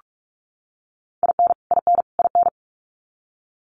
Morse code, heard as a keyed tone, about 44 wpm, 700 Hz. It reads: IN IN IN
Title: Top 500 Words Repeated 45wpm